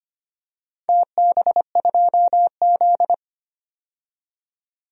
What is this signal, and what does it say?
Morse code, heard as a keyed tone, about 25 wpm, 700 Hz. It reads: TB2Z
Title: DXCC-100 Call Signs - 4 Characters - 25wpm